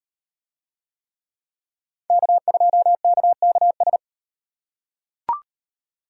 Morse code, heard as a keyed tone, about 38 wpm, 700 Hz. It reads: K2KKS